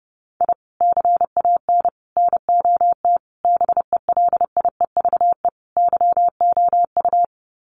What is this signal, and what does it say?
Morse code, heard as a keyed tone, about 30 wpm, 700 Hz. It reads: I CAN NOT BELIEVE YOU
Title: Sets of 5 Words Encore 30wpm